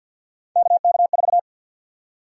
Morse code, heard as a keyed tone, about 51 wpm, 700 Hz. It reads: KK4
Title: USA Call Sign Prefixes - 50wpm